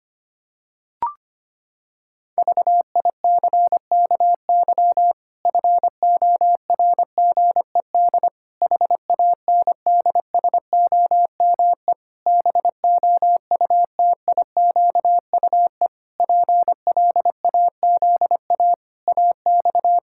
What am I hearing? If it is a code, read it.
Morse code, heard as a keyed tone, about 25 wpm, 700 Hz. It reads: VICKY FORGED HANDSOME BOUTIQUE PLAZA AX
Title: Pangram Challenge v2 - 25wpm